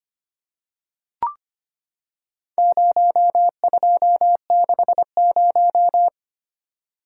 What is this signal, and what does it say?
Morse code, heard as a keyed tone, about 25 wpm, 700 Hz. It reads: 0260
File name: practice / 25wpm / Four Letter-Number Combinations - 25wpm